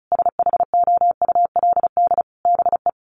Morse code, heard as a keyed tone, about 34 wpm, 700 Hz. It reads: SHOULD BE
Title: Sentences from Top 1300 Words 35wpm